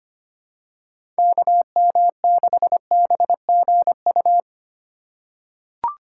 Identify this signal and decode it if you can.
Morse code, heard as a keyed tone, about 25 wpm, 690 Hz. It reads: KM6BGU
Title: USA Call Signs - 6 Characters - 2x Character Spacing - 25wpm